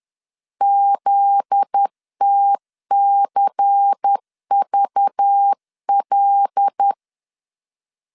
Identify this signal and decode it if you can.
Morse code, heard as a keyed tone, about 10 wpm, 800 Hz. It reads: ZTCVL
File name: CW / DET / 1 9 1500C S A 10wpm 4511